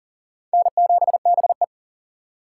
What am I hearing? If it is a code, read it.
Morse code, heard as a keyed tone, about 37 wpm, 700 Hz. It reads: N7BE
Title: USA Call Signs - 4 Characters - 3x Character Spacing - 40wpm